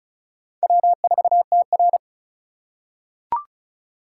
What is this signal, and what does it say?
Morse code, heard as a keyed tone, about 36 wpm, 700 Hz. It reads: W4TR
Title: USA Call Signs - 4 Characters - 4x Character Spacing - 35wpm